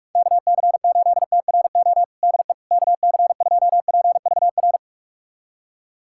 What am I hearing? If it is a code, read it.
Morse code, heard as a keyed tone, about 41 wpm, 700 Hz. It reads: KC8TRO DE KC2PVR
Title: USA Callsign DE Callsign - 6 Characters - 45wpm